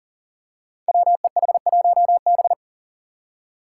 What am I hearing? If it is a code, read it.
Morse code, heard as a keyed tone, about 40 wpm, 700 Hz. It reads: WEH1B
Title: Five Challenging Letter-Number Combinations 40wpm